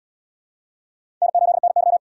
Morse code, heard as a keyed tone, about 77 wpm, 700 Hz. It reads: N8NY